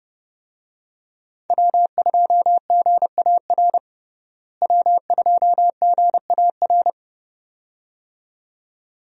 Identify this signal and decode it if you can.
Morse code, heard as a keyed tone, about 30 wpm, 700 Hz. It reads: W2GAR W2GAR
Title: USA Call Signs - 5 Characters - Repeated 2x - 30wpm